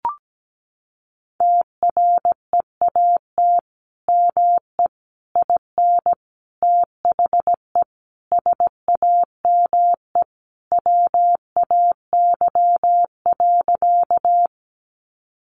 Morse code, heard as a keyed tone, about 17 wpm, 700 Hz. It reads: TREAT ME IN THE SAME WAY.
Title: Sentences from Top 1300 Words 17wpm